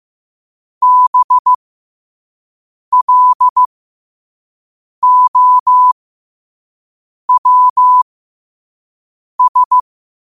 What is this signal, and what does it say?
Morse code, heard as a keyed tone, about 15 wpm, spaced as for 5 wpm, 1000 Hz. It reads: BLOWS
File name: 5wpm